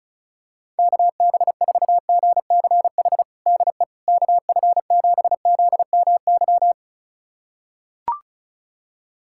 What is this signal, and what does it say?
Morse code, heard as a keyed tone, about 34 wpm, 690 Hz. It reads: KB4GCH DE KF7ZMY
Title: USA Callsign DE Callsign 35wpm